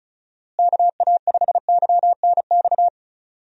Morse code, heard as a keyed tone, about 34 wpm, 700 Hz. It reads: KA5YNX
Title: USA Call Signs - 6 Characters - Repeated 2x - 35wpm